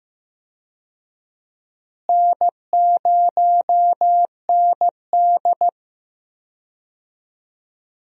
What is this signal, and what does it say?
Morse code, heard as a keyed tone, about 15 wpm, 700 Hz. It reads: N0ND
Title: USA Call Signs - 4 Characters - Repeated 3x - 15wpm